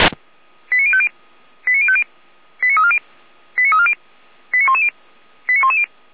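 Five-tone selective call (ZVEI), heard as quick runs of five tones, tones of about 70 ms. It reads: 89050 89050 89340 89340 89110 89110
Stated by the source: ZVEI five-tone call 89097, received via SDR.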